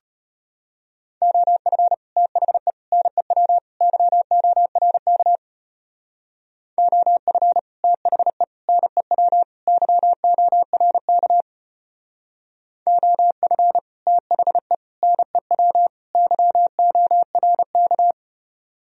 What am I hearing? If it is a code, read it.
Morse code, heard as a keyed tone, about 32 wpm, 700 Hz. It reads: OF THE NEW YORK OF THE NEW YORK OF THE NEW YORK